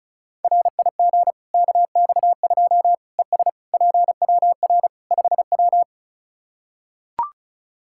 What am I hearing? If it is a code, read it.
Morse code, heard as a keyed tone, about 36 wpm, 700 Hz. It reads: RIG KX2 ES PWR 5W
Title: K1USN SST QSO Practice 35wpm